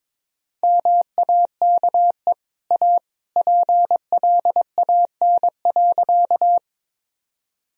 Morse code, heard as a keyed tone, about 22 wpm, 700 Hz. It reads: MAKE A PLAN.
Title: Sentences from Top 1000 Words - Review Part 2 - 22wpm